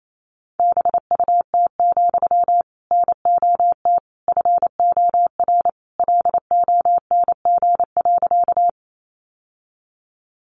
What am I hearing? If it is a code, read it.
Morse code, heard as a keyed tone, about 28 wpm, 700 Hz. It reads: BUT, NOT FOR LONG.
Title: Sentences from Top 600 Words 28wpm